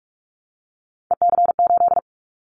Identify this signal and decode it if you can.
Morse code, heard as a keyed tone, about 46 wpm, 700 Hz. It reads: EC8
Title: Callsign Prefixes for Top 100 DXCC - 45wpm